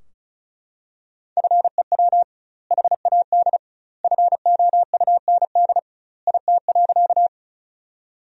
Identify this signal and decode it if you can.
Morse code, heard as a keyed tone, about 36 wpm, 700 Hz. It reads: FEW HAD FOUND IT.
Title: 2x Word-Spacing - Sentences from Top 300 Words - 35wpm